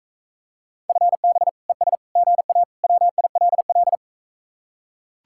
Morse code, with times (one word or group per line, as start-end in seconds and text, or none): 0.9-1.5 FB
1.7-2.0 ES
2.2-2.6 GA
2.8-4.0 WILL